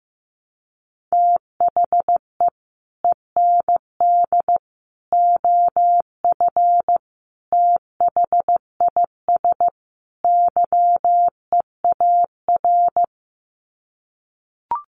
Morse code, heard as a keyed tone, about 14 wpm, 700 Hz. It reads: THE END OF THIS YEAR
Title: Sets of 5 Words Encore 15wpm